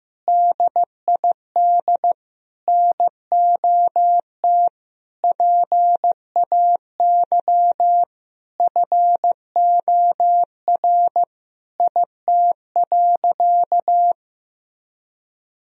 Morse code, heard as a keyed tone, about 15 wpm, 700 Hz. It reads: DID NOT PAY FOR IT.
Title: Sentences from Top 1000 Words - Review Part 3 - 15wpm